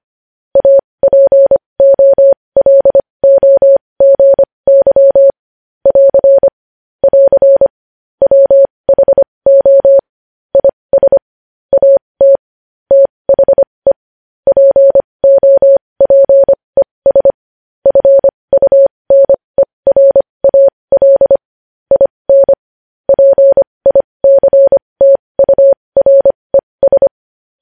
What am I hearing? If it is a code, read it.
Morse code, heard as a keyed tone, about 25 wpm, spaced as for 20 wpm, 550 Hz. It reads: APOLOGY <AR> <AR> WHO IS AT THE POPES FUNERAL IN PICTURES
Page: news headlines in morse code at 20 wpm